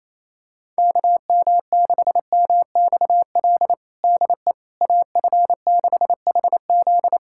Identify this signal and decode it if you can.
Morse code, heard as a keyed tone, about 28 wpm, 700 Hz. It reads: KM6MXL DE AF6HZ
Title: USA Callsign DE Callsign 28wpm